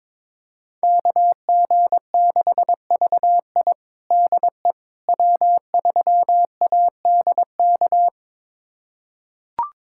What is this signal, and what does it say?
Morse code, heard as a keyed tone, about 22 wpm, 700 Hz. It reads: KG6VI DE W3ADK
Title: USA Callsign DE Callsign - 5 Characters - 22wpm